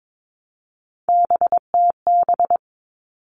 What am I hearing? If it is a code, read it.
Morse code, heard as a keyed tone, about 22 wpm, 700 Hz. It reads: BTB
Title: All Cut Numbers - 3-Digit 22wpm